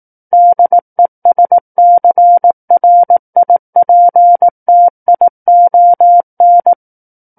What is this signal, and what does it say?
Morse code, heard as a keyed tone, about 18 wpm, 700 Hz. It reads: DESCRIPTION
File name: 18 WPM